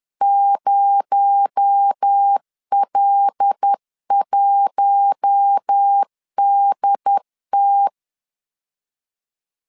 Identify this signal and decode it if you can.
Morse code, heard as a keyed tone, about 10 wpm, 800 Hz. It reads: 0L1DT
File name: LEZIONI / 8 YJ45 / 8 Y 10wpm 2615